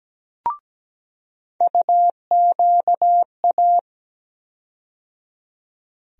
Morse code, heard as a keyed tone, about 17 wpm, 700 Hz. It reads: UQA
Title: USA Call Sign Postfixes - 3 Letters 17wpm